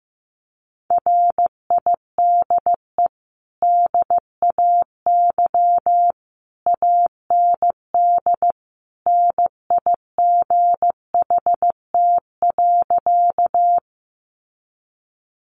Morse code, heard as a keyed tone, about 15 wpm, 700 Hz. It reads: RIDE DAY AND NIGHT.
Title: Sentences from Top 1200 Words 15wpm